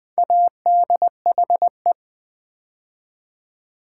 Morse code, heard as a keyed tone, about 20 wpm, 700 Hz. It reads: ADHE
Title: Build A Word - 2 Letters At A Time - 6-Letter Words - 20wpm